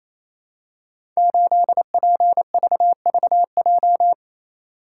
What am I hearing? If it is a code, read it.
Morse code, heard as a keyed tone, about 27 wpm, 700 Hz. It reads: 8PVVJ
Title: Code Groups - 5 Letter:Numbers - 28wpm